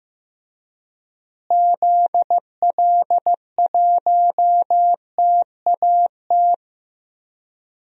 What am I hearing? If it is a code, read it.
Morse code, heard as a keyed tone, about 15 wpm, 700 Hz. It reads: ZL1TAT
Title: DXCC-100 Call Signs - 6 Characters - Repeated 3x - 15wpm